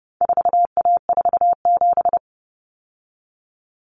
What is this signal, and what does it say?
Morse code, heard as a keyed tone, about 30 wpm, 700 Hz. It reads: 4A47
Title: Common Cut Numbers - 4-Digit 30wpm